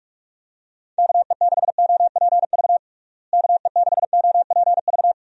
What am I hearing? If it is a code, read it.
Morse code, heard as a keyed tone, about 42 wpm, 700 Hz. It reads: KE6OPV KE6OPV
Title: USA Call Signs - 6 Characters - Repeated 2x - 45wpm